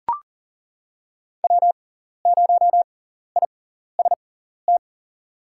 Morse code, heard as a keyed tone, about 40 wpm, spaced as for 14 wpm, 700 Hz. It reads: W0IST
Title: USA Call Signs - 5 Characters - 6x Character Spacing - 40wpm